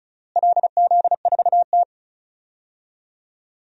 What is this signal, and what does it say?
Morse code, heard as a keyed tone, about 36 wpm, 700 Hz. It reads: LZ4T
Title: DXCC-100 Call Signs - 4 Characters - Repeated 3x - 35wpm